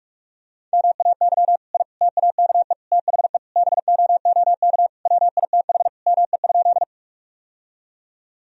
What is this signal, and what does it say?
Morse code, heard as a keyed tone, about 46 wpm, 700 Hz. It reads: MAY I TAKE THE BOOK WITH ME?
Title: Sentences from Top 1000 Words - Review Part 1 - 45wpm